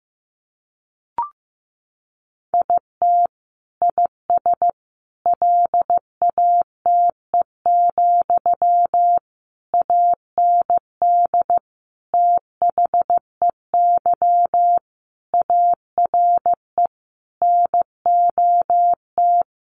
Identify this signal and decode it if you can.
Morse code, heard as a keyed tone, about 15 wpm, 700 Hz. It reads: IT IS LATE, AND THEY ARE NOT
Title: Sentences from Top 600 Words 15wpm